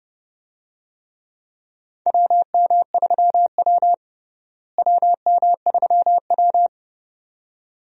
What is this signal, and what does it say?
Morse code, heard as a keyed tone, about 31 wpm, 700 Hz. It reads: WM3W WM3W
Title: USA Call Signs - 4 Characters - Repeated 2x - 30wpm